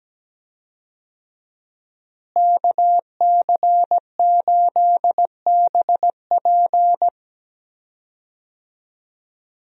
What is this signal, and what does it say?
Morse code, heard as a keyed tone, about 17 wpm, 700 Hz. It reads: KC8BP
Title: USA Call Signs - 5 Characters - Repeated 2x - 17wpm